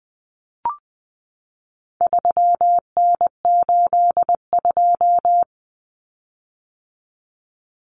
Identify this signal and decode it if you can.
Morse code, heard as a keyed tone, about 20 wpm, 700 Hz. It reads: 3N82
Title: Common Cut Numbers - 4-Digit 20wpm